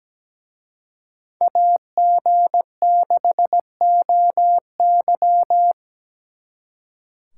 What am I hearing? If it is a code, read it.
Morse code, heard as a keyed tone, about 20 wpm, 700 Hz. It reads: AG6OY
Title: Speed Racing - USA 2x2 Call Signs 17wpm